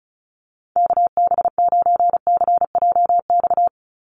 Morse code, heard as a keyed tone, about 36 wpm, 700 Hz. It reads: KB9CJX
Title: USA Call Signs 35wpm